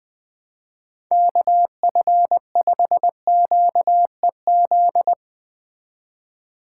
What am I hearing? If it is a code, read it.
Morse code, heard as a keyed tone, about 20 wpm, 700 Hz. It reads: KF5QEZ